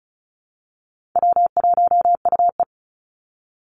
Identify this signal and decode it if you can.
Morse code, heard as a keyed tone, about 34 wpm, 700 Hz. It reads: W1UE
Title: Call Signs - 35wpm